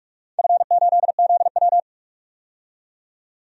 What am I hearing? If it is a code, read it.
Morse code, heard as a keyed tone, about 46 wpm, 700 Hz. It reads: F8ZW
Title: DXCC-100 Call Signs - 4 Characters - 45wpm